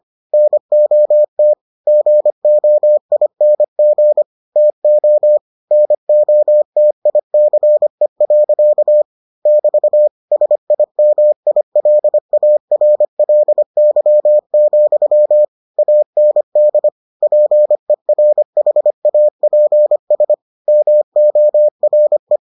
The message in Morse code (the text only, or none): NOT GOING TO NOTICE. <BT> SIMILARLY, AND PERHAPS MORE